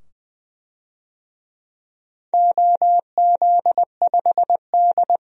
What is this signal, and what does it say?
Morse code, heard as a keyed tone, about 20 wpm, 700 Hz. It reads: OZ5D
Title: DXCC-100 Call Signs - 4 Characters - 20wpm